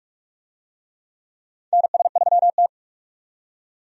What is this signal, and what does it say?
Morse code, heard as a keyed tone, about 44 wpm, 700 Hz. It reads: NS3T
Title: USA Call Signs - 4 Characters - 45wpm